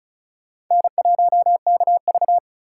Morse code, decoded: N1KV

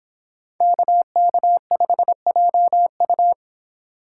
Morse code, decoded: KK5JU

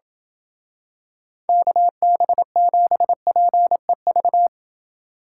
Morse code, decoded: KB7PEV

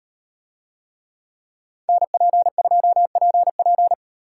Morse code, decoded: NP2PP

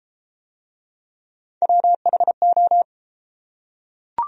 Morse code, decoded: WHO

